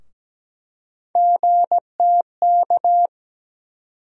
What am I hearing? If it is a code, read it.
Morse code, heard as a keyed tone, about 17 wpm, 700 Hz. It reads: GTK
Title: USA Call Sign Postfixes - 3 Letters 17wpm